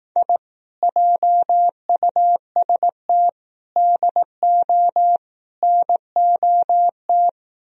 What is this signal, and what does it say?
Morse code, heard as a keyed tone, about 18 wpm, 700 Hz. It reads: I JUST DO NOT
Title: Speed-Racing - Sets of 4 Words 15wpm